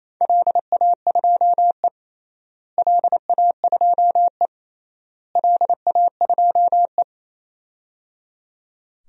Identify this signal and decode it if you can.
Morse code, heard as a keyed tone, about 28 wpm, 700 Hz. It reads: LA2E LA2E LA2E
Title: DXCC-100 Call Signs - 4 Characters - Repeated 3x - 28wpm